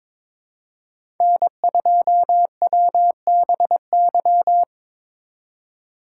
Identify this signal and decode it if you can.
Morse code, heard as a keyed tone, about 22 wpm, 700 Hz. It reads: N2WBY
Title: USA Call Signs - 5 Characters - 2x Character Spacing - 22wpm